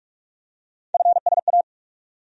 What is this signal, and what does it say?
Morse code, heard as a keyed tone, about 42 wpm, 700 Hz. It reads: FSA
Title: USA Call Sign Postfixes - 3 Letters 45wpm